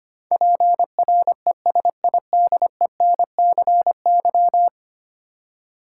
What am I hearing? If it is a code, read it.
Morse code, heard as a keyed tone, about 25 wpm, 700 Hz. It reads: PRESIDENCY
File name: 10 Letter Words - First 3 Letters Given - 25wpm